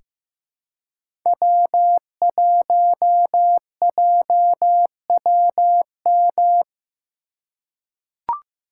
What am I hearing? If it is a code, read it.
Morse code, heard as a keyed tone, about 15 wpm, 700 Hz. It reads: W1JWM
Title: USA Call Signs - 5 Characters - 4x Character Spacing - 15wpm